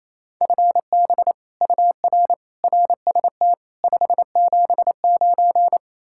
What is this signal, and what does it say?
Morse code, heard as a keyed tone, about 35 wpm, 700 Hz. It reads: FB UR RST 579
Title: Speed-Racing Common QSO Phrases - 28wpm